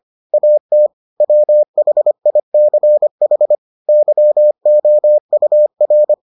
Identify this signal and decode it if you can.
Morse code, heard as a keyed tone, about 25 wpm, 600 Hz. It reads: AT WHICH YOUR